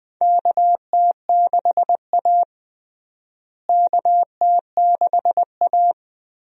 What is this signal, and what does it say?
Morse code, heard as a keyed tone, about 20 wpm, 700 Hz. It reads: KT6A KT6A
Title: USA Call Signs - 4 Characters - Repeated 2x - 20wpm